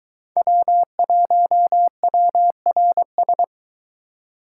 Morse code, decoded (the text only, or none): W1WRS